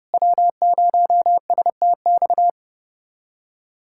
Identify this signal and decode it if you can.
Morse code, heard as a keyed tone, about 30 wpm, 700 Hz. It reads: W0STX
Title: USA Callsign DE Callsign - 5 Characters - 30wpm